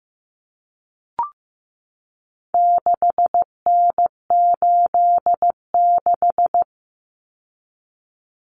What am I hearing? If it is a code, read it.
Morse code, heard as a keyed tone, about 15 wpm, 700 Hz. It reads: 6N86
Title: Common Cut Numbers - 4-Digit 15wpm